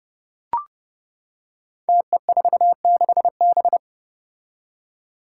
Morse code, heard as a keyed tone, about 32 wpm, 700 Hz. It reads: TE46B
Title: Five Challenging Letter-Number Combinations 30wpm